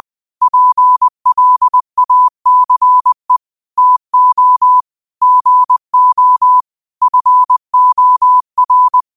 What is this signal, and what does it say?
Morse code, heard as a keyed tone, about 20 wpm, 1000 Hz. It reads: PLACE TO GO FOR